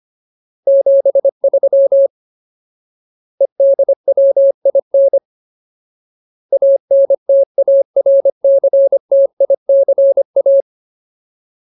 Morse code, heard as a keyed tone, about 25 wpm, 550 Hz. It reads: 73 EDWIN ANTARCTICA
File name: Ultimate Word List / 25WPM Ultimate